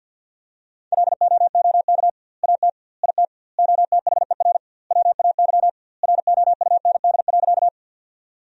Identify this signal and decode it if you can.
Morse code, heard as a keyed tone, about 49 wpm, 700 Hz. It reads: LOOK AT IT OTHER WAY ROUND.
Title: 2x Word-Spacing - Sentences from Top 200 Words - 50wpm